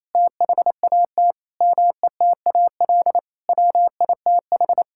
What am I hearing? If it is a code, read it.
Morse code, heard as a keyed tone, about 28 wpm, 700 Hz. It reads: THAT METAL WITH
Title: Sentences from Top 1000 Words - Review Part 6 - 28wpm